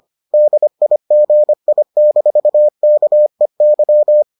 Morse code, heard as a keyed tone, about 25 wpm, 600 Hz. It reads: DIGI-KEY